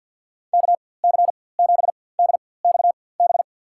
Morse code, heard as a keyed tone, about 50 wpm, spaced as for 23 wpm, 700 Hz. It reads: KC7DXB
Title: USA Call Signs - 6 Characters - 4x Character Spacing - 50wpm